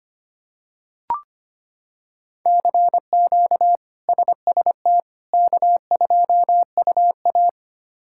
Morse code, heard as a keyed tone, about 25 wpm, 700 Hz. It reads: CQ SST K2UA
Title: K1USN SST QSO Practice - 25wpm